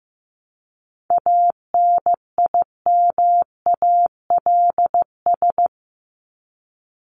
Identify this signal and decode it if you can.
Morse code, heard as a keyed tone, about 15 wpm, 700 Hz. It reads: ANIMALS
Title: ES:S Suffix - Top 100 Words with Base Words 15wpm